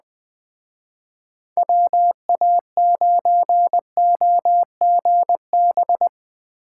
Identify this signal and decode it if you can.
Morse code, heard as a keyed tone, about 21 wpm, 700 Hz. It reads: WA9OGB
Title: Speed Racing - USA 2x3 Call Signs 17wpm